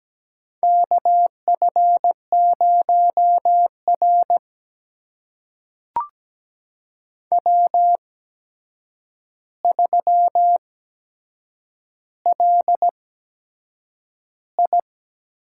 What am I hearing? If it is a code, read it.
Morse code, heard as a keyed tone, about 17 wpm, spaced as for 7 wpm, 700 Hz. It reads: KF0R W3LI
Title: USA Call Signs - 4 Characters - 8x Character Spacing - 17wpm